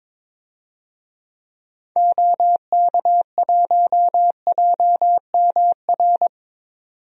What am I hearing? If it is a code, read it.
Morse code, heard as a keyed tone, about 22 wpm, 700 Hz. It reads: OK1JMR
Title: DXCC-100 Call Signs - 6 Characters - 22wpm